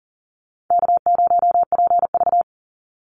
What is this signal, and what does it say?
Morse code, heard as a keyed tone, about 40 wpm, 700 Hz. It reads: K0PV
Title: Easy Call Signs - 40wpm